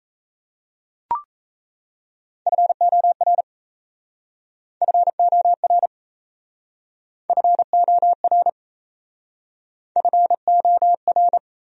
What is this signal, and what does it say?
Morse code, heard as a keyed tone, about 37 wpm, 700 Hz. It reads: FOR FOR FOR FOR